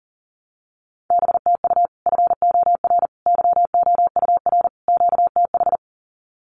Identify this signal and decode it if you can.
Morse code, heard as a keyed tone, about 37 wpm, 700 Hz. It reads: BTU FOR YOUR QTH
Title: Common QSO Phrases 40wpm